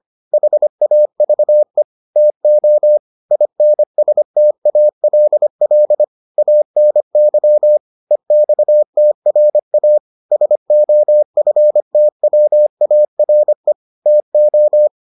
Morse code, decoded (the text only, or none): HAVE TO INSTALL ANY EXTRA SOFTWARE TO